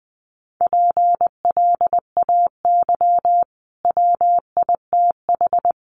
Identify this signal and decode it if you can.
Morse code, heard as a keyed tone, about 20 wpm, 700 Hz. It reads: PLAY WITH
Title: Sentences from Top 200 Words 20wpm